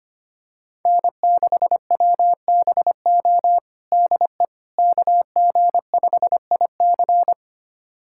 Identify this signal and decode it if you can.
Morse code, heard as a keyed tone, about 25 wpm, 700 Hz. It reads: N6WBO DE KG5IC
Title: USA Callsign DE Callsign - 5 Characters - 25wpm